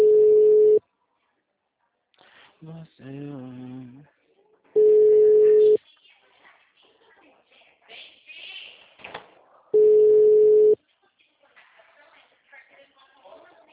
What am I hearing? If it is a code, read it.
Ringback tone (Europe, 425 Hz), with speech.
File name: out- -2014- - - .8316